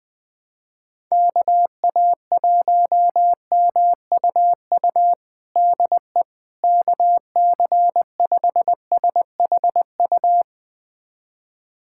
Morse code, heard as a keyed tone, about 20 wpm, 700 Hz. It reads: KA1MUU DE KC5SHU
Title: USA Callsign DE Callsign - 6 Characters - 20wpm